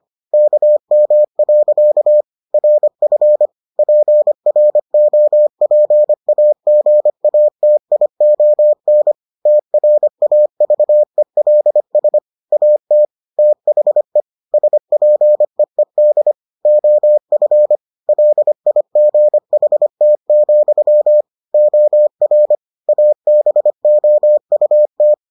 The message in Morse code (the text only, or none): KM. RF PROPAGATION TRAVELS AT THE SPEED OF LIGHT, OR ABOUT